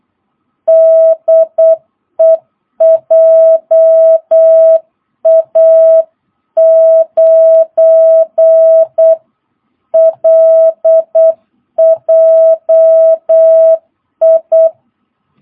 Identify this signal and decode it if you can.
Morse code, heard as a keyed tone, about 8 wpm, 660 Hz. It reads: DE JA9LJI